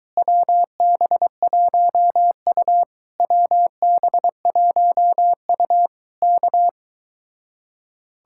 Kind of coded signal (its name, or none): Morse code